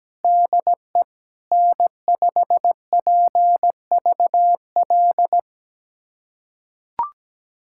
Morse code, heard as a keyed tone, about 17 wpm, 700 Hz. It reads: DE N5PVL
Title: USA Callsign DE Callsign 17wpm